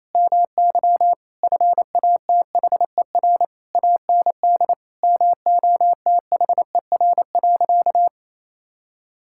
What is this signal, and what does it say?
Morse code, heard as a keyed tone, about 28 wpm, 700 Hz. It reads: MY FATHER AND MOTHER.
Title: Sentences from Top 1600 Words 28wpm